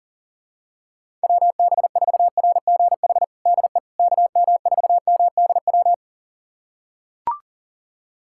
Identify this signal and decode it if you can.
Morse code, heard as a keyed tone, about 40 wpm, 700 Hz. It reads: WB4RGH DE KM4MDW
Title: USA Callsign DE Callsign 40wpm